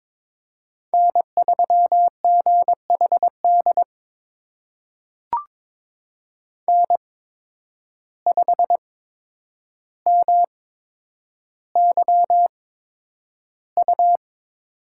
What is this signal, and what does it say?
Morse code, heard as a keyed tone, about 22 wpm, spaced as for 9 wpm, 700 Hz. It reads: N3GHD N5MYU
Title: USA Call Signs - 5 Characters - 8x Character Spacing - 22wpm